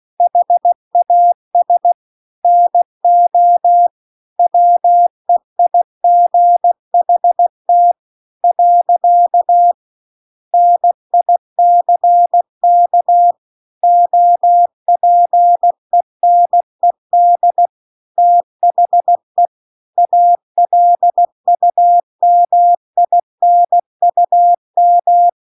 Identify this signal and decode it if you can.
Morse code, heard as a keyed tone, about 16 wpm, 700 Hz. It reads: HAS NO WEIGHT . NICK OPENED THE ALUMINUM
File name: CW-The-Big-Twoheaded-River-2-16WPM